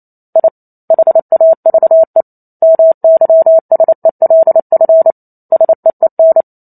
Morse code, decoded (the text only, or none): I HAVE MYSELF SEEN